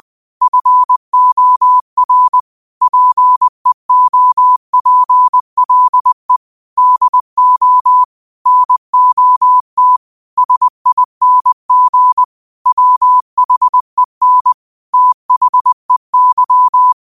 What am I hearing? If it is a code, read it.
Morse code, heard as a keyed tone, about 20 wpm, 1000 Hz. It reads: FOR PEOPLE DO NOT SING WHEN THEY